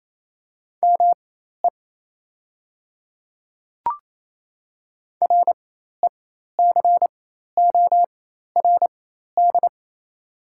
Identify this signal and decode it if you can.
Morse code, heard as a keyed tone, about 28 wpm, spaced as for 13 wpm, 700 Hz. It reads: ME RECORD
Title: 4x Character Spacing - Top 500 Words - 28wpm